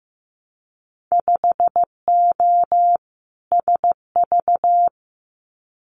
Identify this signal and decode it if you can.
Morse code, heard as a keyed tone, about 15 wpm, 700 Hz. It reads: 5O SV
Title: Winter Field Day Exchanges Encore 15wpm